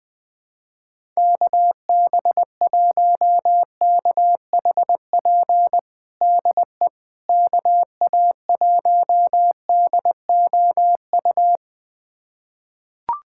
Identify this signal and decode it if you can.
Morse code, heard as a keyed tone, about 21 wpm, 700 Hz. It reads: KB1KHP DE KA1DOU
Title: USA Callsign DE Callsign 20wpm